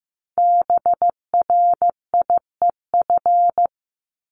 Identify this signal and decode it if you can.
Morse code, heard as a keyed tone, about 15 wpm, 700 Hz. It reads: BRIEF
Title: LY Suffix - Top 100 Words with Base Words 15wpm